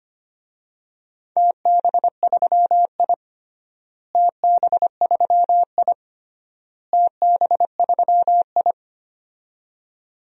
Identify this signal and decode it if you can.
Morse code, heard as a keyed tone, about 25 wpm, 700 Hz. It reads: TB3I TB3I TB3I
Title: DXCC-100 Call Signs - 4 Characters - Repeated 3x - 25wpm